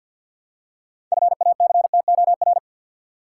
Morse code, heard as a keyed tone, about 50 wpm, 700 Hz. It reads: FACTOR
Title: Six Letter Words - 50wpm